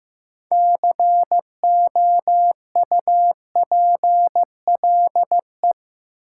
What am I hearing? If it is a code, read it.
Morse code, heard as a keyed tone, about 15 wpm, 700 Hz. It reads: COUPLE
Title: Sets of 2 Words - New Word Drill - 15wpm